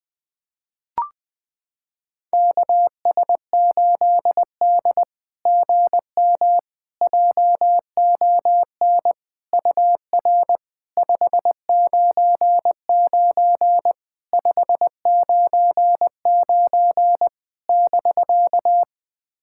Morse code, heard as a keyed tone, about 20 wpm, 700 Hz. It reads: KS8D GM JON UR 599 599 <BK>
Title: SOTA QSO Practice 20wpm